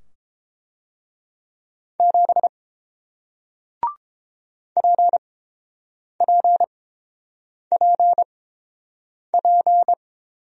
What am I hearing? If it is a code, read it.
Morse code, heard as a keyed tone, about 30 wpm, spaced as for 7 wpm, 690 Hz. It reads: 7 PPPP